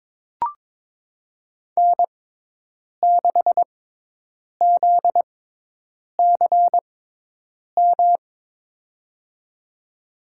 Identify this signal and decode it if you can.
Morse code, heard as a keyed tone, about 22 wpm, spaced as for 8 wpm, 700 Hz. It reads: N6ZCM
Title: USA Call Signs - 5 Characters - 6x Character Spacing - 22wpm